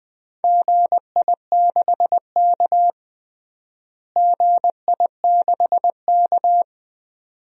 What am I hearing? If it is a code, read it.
Morse code, heard as a keyed tone, about 20 wpm, 700 Hz. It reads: GI6K GI6K
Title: DXCC-100 Call Signs - 4 Characters - Repeated 2x - 20wpm